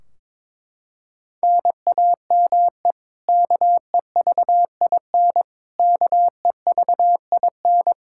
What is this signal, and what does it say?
Morse code, heard as a keyed tone, about 22 wpm, 700 Hz. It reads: NAME KEVIN KEVIN
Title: Practice QSO - Exchange 1 - Beginning - 22wpm